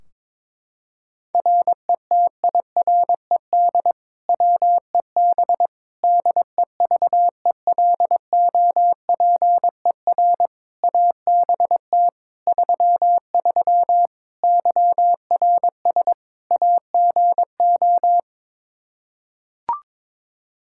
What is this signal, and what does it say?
Morse code, heard as a keyed tone, about 22 wpm, 700 Hz. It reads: RETIRED WEB DEVELOPER ABT 33 YRS AGO
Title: K1USN SST QSO Practice 22wpm